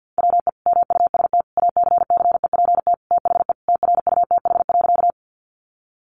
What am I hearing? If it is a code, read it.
Morse code, heard as a keyed tone, about 50 wpm, 700 Hz. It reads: WE MUST ACCEPT THE TRUTH.
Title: Sentences from Top 1300 Words 50wpm